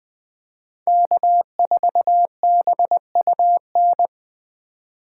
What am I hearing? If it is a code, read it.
Morse code, heard as a keyed tone, about 20 wpm, 700 Hz. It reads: K4BUN
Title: USA Call Signs - 3x Character Spacing - 20wpm